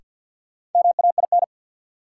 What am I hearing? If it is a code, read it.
Morse code, heard as a keyed tone, about 53 wpm, 700 Hz. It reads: MAIN
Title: Sentences from Top 1000 Words - Review Part 2 - New Word Drill - 50wpm